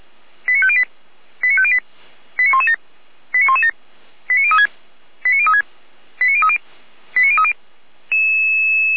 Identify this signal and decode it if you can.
Five-tone selective call (ZVEI), heard as quick runs of five tones, tones of about 70 ms. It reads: 89508 89508 89117 89117 89036 89036 89030 89030